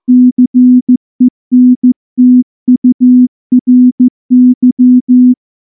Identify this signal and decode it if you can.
Morse code, heard as a keyed tone, about 15 wpm, 250 Hz. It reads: CENTURY
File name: morse / morse-training-15WPM